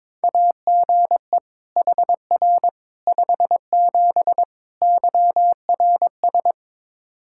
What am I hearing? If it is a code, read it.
Morse code, heard as a keyed tone, about 22 wpm, 700 Hz. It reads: AGE HR 57 YRS
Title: Practice QSO - Exchange 3 - More Info - 22wpm